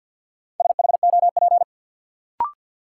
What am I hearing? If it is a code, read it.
Morse code, heard as a keyed tone, about 53 wpm, 700 Hz. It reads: SHOP
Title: Sentences from Top 1600 Words - New Word Drill - 50wpm